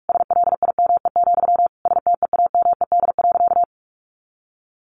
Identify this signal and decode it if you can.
Morse code, heard as a keyed tone, about 46 wpm, 700 Hz. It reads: SLIME, STEAMED.
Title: A Sound Of Thunder 45wpm